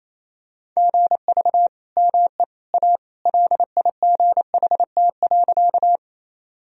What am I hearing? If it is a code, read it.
Morse code, heard as a keyed tone, about 28 wpm, 700 Hz. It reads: GV ME A LIGHT.
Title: Sentences from Top 200 Words and Abbreviations 28wpm